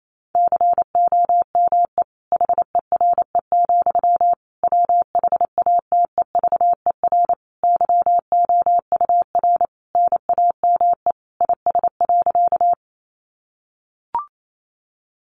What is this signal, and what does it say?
Morse code, heard as a keyed tone, about 28 wpm, 700 Hz. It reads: COME HERE, WHATEVER YOUR NAME IS.
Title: Sentences from Top 1100 Words 28wpm